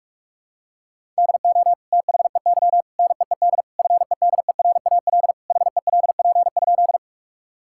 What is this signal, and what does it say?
Morse code, heard as a keyed tone, about 46 wpm, 700 Hz. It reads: DO THEY NEED FEDERAL HELP?
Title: Sentences from Top 1000 Words - Review Part 5 - 45wpm